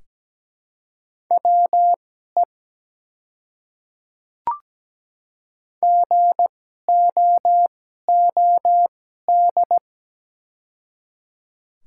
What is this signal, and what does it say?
Morse code, heard as a keyed tone, about 17 wpm, spaced as for 12 wpm, 700 Hz. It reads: WE GOOD
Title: 2x Character Spacing - Top 200 Words - 17wpm